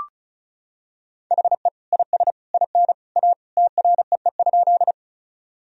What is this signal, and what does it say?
Morse code, heard as a keyed tone, about 34 wpm, 700 Hz. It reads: HE IS IN A TREE?
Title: Sentences from Top 1000 Words - Review Part 1 - 35wpm